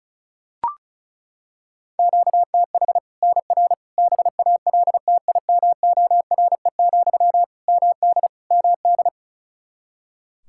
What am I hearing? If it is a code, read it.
Morse code, heard as a keyed tone, about 36 wpm, 700 Hz. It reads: QTH NR BALTIMORE, MD MD
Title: Practice QSO - Exchange 1 - Beginning - 35wpm